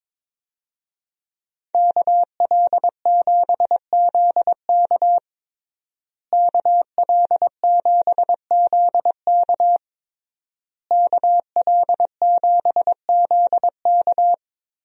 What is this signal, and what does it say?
Morse code, heard as a keyed tone, about 22 wpm, 700 Hz. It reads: KL7ZK KL7ZK KL7ZK
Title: USA Call Signs - 5 Characters - Repeated 3x - 22wpm